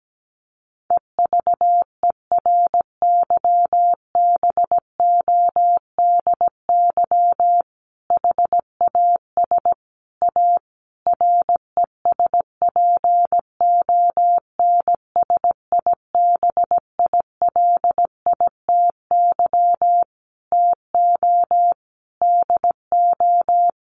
Morse code, heard as a keyed tone, about 17 wpm, 700 Hz. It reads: EVERYBODY HAS A RESPONSIBILITY TO DO
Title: Sentences from Top 1400 Words 17wpm